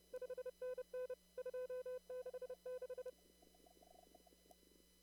Morse code, heard as a keyed tone, about 29 wpm, 500 Hz. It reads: 5NN 266
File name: CW / kt5j wpx cw-op-k5pi